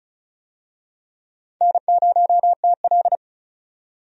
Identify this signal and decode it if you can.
Morse code, heard as a keyed tone, about 36 wpm, 700 Hz. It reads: N0TL
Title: USA Call Signs - 4 Characters - 35wpm